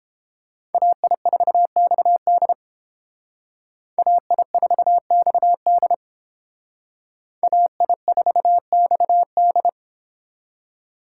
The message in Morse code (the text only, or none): AI4XD AI4XD AI4XD